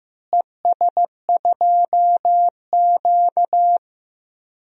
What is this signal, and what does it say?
Morse code, heard as a keyed tone, about 15 wpm, 700 Hz. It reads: ES2Q
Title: DXCC-100 Call Signs - 4 Characters - Repeated 2x - 15wpm